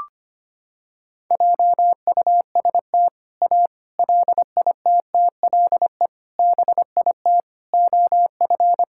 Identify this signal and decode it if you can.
Morse code, heard as a keyed tone, about 25 wpm, 700 Hz. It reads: JUST A LITTLE BIT OF
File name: practice / 25wpm / Sentences with 4-Word Phrases - 25wpm